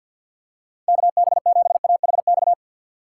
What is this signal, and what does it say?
Morse code, heard as a keyed tone, about 52 wpm, 700 Hz. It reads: KB7AHX
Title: USA Call Signs - 6 Characters - 3x Character Spacing - 50wpm